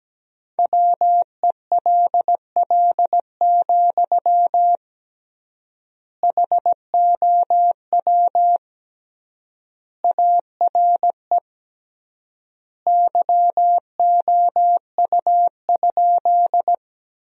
Morse code, heard as a keyed tone, about 17 wpm, 700 Hz. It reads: WELL, HOW ARE YOU?
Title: 3x Word-Spacing - Sentences from Top 200 Words - 17wpm